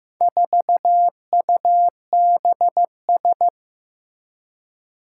Morse code, heard as a keyed tone, about 15 wpm, 700 Hz. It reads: 4UBS
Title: Four Challenging Letter-Number Combinations 15wpm